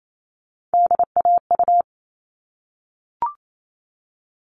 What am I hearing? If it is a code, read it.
Morse code, heard as a keyed tone, about 28 wpm, 700 Hz. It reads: DAU
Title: All Cut Numbers - 3-Digit 28wpm